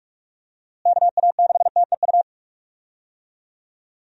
Morse code, heard as a keyed tone, about 46 wpm, 700 Hz. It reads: KA6TEU
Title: USA Call Signs - 6 Characters - Repeated 2x - 45wpm